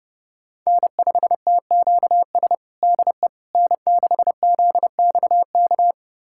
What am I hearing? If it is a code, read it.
Morse code, heard as a keyed tone, about 32 wpm, 700 Hz. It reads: N5TQS DE N6ZXK
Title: USA Callsign DE Callsign - 5 Characters - 30wpm